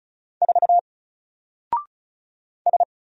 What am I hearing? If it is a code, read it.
Morse code, heard as a keyed tone, about 35 wpm, 700 Hz. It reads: 4 S
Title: Single Character (Letter:Number:Punctuation:Prosign) 35wpm